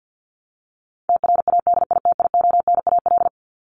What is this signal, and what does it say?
Morse code, heard as a keyed tone, about 50 wpm, 700 Hz. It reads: TRADITIONAL
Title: Top 751-1000 Words 50wpm